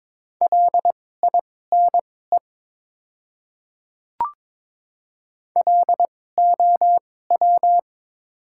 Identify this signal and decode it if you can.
Morse code, heard as a keyed tone, about 22 wpm, spaced as for 16 wpm, 700 Hz. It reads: LINE LOW
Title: 2x Character Spacing - Top 500 Words - 22wpm